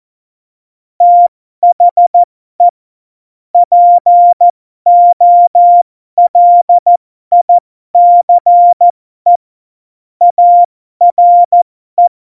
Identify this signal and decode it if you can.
Morse code, heard as a keyed tone, about 14 wpm, spaced as for 12 wpm, 700 Hz. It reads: THE POLICE ARE
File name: Morsekurs / Witze